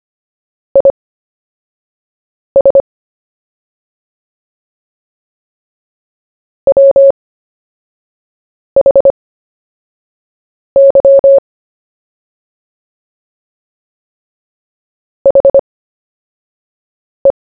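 Morse code, heard as a keyed tone, about 25 wpm, spaced as for 5 wpm, 550 Hz. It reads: IS WHY HE
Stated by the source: CW story-telling practice, Short Story 1 sent at 5 wpm, by VK4JE, Queensland, Australia.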